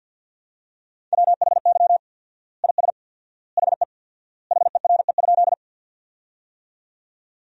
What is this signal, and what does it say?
Morse code, heard as a keyed tone, about 50 wpm, 700 Hz. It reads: WHY IS HE HERE?
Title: 4x Word-Spacing - Sentences from Top 200 Words - 50wpm